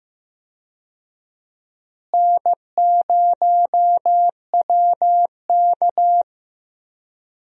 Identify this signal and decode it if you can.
Morse code, heard as a keyed tone, about 15 wpm, 700 Hz. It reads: N0WK
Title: USA Call Signs - 4 Characters - Repeated 2x - 15wpm